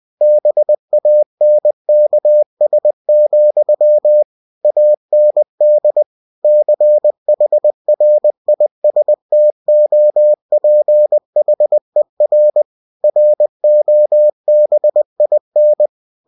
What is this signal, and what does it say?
Morse code, heard as a keyed tone, about 20 wpm, 600 Hz. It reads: BANKS, AND CHRISTOPHER ROBIN
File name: cw-books / winnie-the-pooh / winnie-20wpm-page-0082